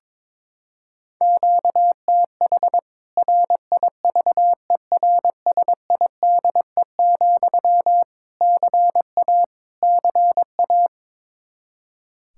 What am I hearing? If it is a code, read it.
Morse code, heard as a keyed tone, about 21 wpm, 700 Hz. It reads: QTH RIVERSIDE, CA CA
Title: K1USN SST QSO Practice 22wpm